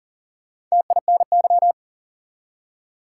Morse code, heard as a keyed tone, about 38 wpm, 700 Hz. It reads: TINY
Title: Four Letter Words - 40wpm